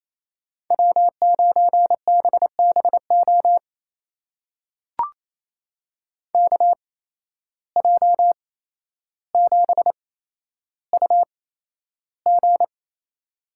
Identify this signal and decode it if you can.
Morse code, heard as a keyed tone, about 28 wpm, spaced as for 12 wpm, 700 Hz. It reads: W9BBO KJ7UG
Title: USA Call Signs - 5 Characters - 8x Character Spacing - 28wpm